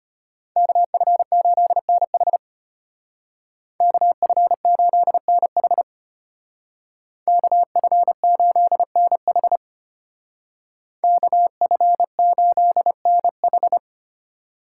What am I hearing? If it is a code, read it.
Morse code, heard as a keyed tone, about 33 wpm, 700 Hz. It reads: KF8NH KF8NH KF8NH KF8NH